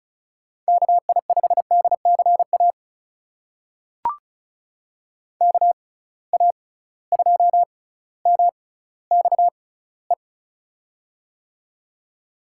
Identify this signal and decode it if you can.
Morse code, heard as a keyed tone, about 36 wpm, spaced as for 18 wpm, 700 Hz. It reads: KI5DCA KA2MXE
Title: USA Call Signs - 6 Characters - 6x Character Spacing - 35wpm